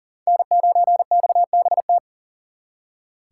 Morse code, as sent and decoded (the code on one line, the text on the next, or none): -. ----. -..- -... -
N9XBT